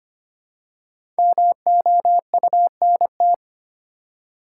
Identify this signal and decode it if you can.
Morse code, heard as a keyed tone, about 25 wpm, 700 Hz. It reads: MOUNT
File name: 5 Letter Words - First 3 Letters Given - 25wpm